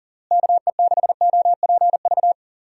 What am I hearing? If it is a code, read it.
Morse code, heard as a keyed tone, about 37 wpm, 700 Hz. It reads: KE6OPV
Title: USA Call Signs - 6 Characters - Repeated 2x - 40wpm